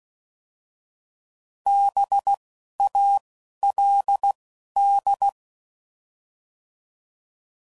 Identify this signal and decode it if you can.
Morse code, heard as a keyed tone, about 16 wpm, 800 Hz. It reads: B A L D